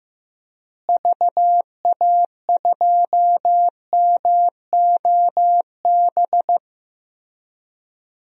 Morse code, heard as a keyed tone, about 15 wpm, 700 Hz. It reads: VA2MOB
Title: DXCC-100 Call Signs - 6 Characters - Repeated 3x - 15wpm